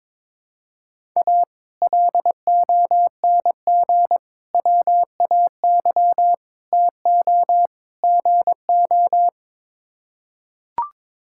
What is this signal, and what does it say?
Morse code, heard as a keyed tone, about 21 wpm, 700 Hz. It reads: A LONG WAY TO GO
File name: practice / 22wpm / Build A Phrase - 5-Word Phrases - 22wpm